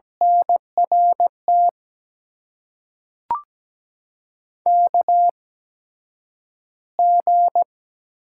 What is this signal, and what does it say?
Morse code, heard as a keyed tone, about 17 wpm, spaced as for 9 wpm, 700 Hz. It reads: NRT KG
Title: USA Call Signs - 6 Characters - 8x Character Spacing - 17wpm